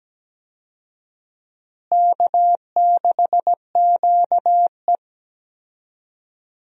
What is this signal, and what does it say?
Morse code, heard as a keyed tone, about 17 wpm, 700 Hz. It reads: K6QE
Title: USA Call Signs - 4 Characters - Repeated 3x - 17wpm